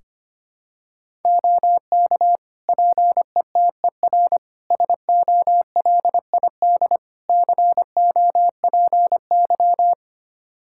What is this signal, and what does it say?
Morse code, heard as a keyed tone, about 25 wpm, 700 Hz. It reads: OK PETER SOLID COPY
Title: K1USN SST QSO Practice 25wpm